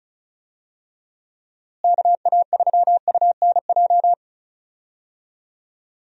Morse code, heard as a keyed tone, about 36 wpm, 700 Hz. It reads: KA3UNJ
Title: USA Call Signs - 6 Characters - Repeated 3x - 35wpm